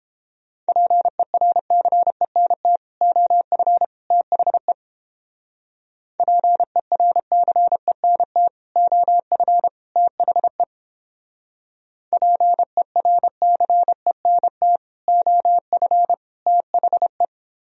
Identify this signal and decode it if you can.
Morse code, heard as a keyed tone, about 29 wpm, 700 Hz. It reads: PERCENT OF THE PERCENT OF THE PERCENT OF THE